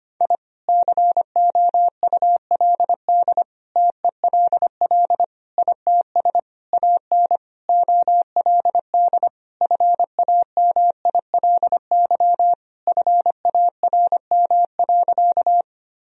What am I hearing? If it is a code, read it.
Morse code, heard as a keyed tone, about 25 wpm, 700 Hz. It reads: I COULD TELL ITS AN OLD FAMILY FARM.
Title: Sentences from Top 1000 Words - Review Part 2 - 25wpm